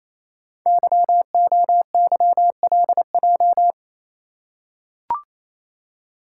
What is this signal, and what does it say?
Morse code, heard as a keyed tone, about 29 wpm, 700 Hz. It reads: YOYLJ
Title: Code Groups - 5 Letter:Numbers - 28wpm